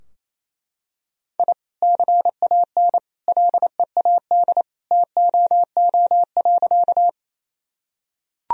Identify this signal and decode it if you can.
Morse code, heard as a keyed tone, about 28 wpm, 700 Hz. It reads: I CAN LEAD TOO.
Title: Sentences from Top 1000 Words - Review Part 2 - 28wpm